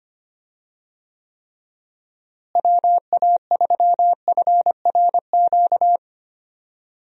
Morse code, heard as a keyed tone, about 25 wpm, 700 Hz. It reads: WA3FRQ
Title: USA Call Signs - 6 Characters - Repeated 2x - 25wpm